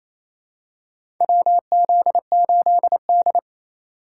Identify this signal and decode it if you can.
Morse code, heard as a keyed tone, about 28 wpm, 700 Hz. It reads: WZ8D
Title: USA Call Signs - 4 Characters - 3x Character Spacing - 28wpm